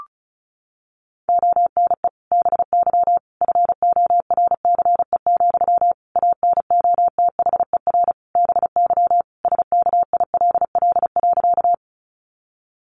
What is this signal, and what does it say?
Morse code, heard as a keyed tone, about 34 wpm, 700 Hz. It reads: ONE BY FORCE, ANOTHER BY SKILL.
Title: Sentences from Top 1200 Words 35wpm